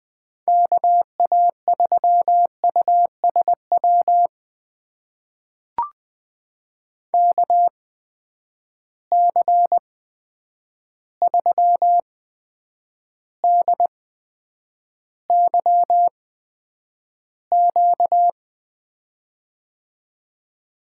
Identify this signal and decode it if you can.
Morse code, heard as a keyed tone, about 20 wpm, spaced as for 9 wpm, 700 Hz. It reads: KA3USW KC3DYQ
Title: USA Call Signs - 6 Characters - 8x Character Spacing - 20wpm